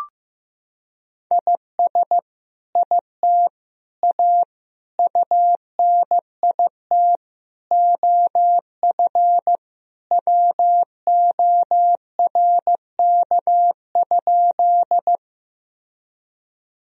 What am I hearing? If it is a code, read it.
Morse code, heard as a keyed tone, about 15 wpm, 700 Hz. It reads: IS IT A UNIT OF WORK?
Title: Sentences from Top 1000 Words - Review Part 2 - 15wpm